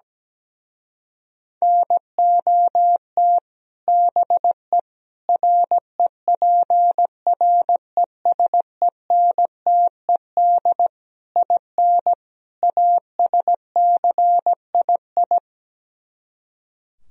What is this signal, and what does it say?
Morse code, heard as a keyed tone, about 17 wpm, 700 Hz. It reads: NOT BE REPRESENTED IN ASCII
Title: Speed-Racing - Sets of 5 Words 17wpm